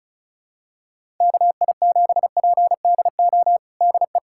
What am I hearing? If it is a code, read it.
Morse code, heard as a keyed tone, about 34 wpm, 700 Hz. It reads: KI7PDO DE
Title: USA Callsign DE Callsign - 6 Characters - 35wpm